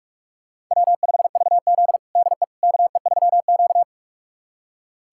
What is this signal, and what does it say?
Morse code, heard as a keyed tone, about 44 wpm, 700 Hz. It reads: W5VZ DE KE3Q
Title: USA Callsign DE Callsign - 4 Characters - 45wpm